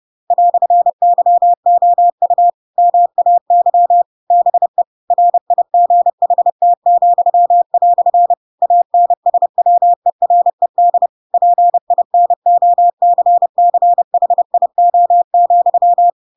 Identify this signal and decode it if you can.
Morse code, heard as a keyed tone, about 30 wpm, 700 Hz. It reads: "YOU MAY BE RIGHT," ANSWERED PINOCCHIO,
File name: Pinocchio / Pinocchio ch7 30 30 ik1hge